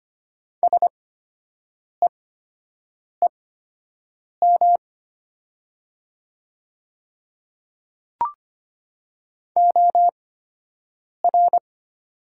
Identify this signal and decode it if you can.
Morse code, heard as a keyed tone, about 25 wpm, spaced as for 7 wpm, 690 Hz. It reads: SEEM OR
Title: 8x Character Spacing - Top 300 Words - 25wpm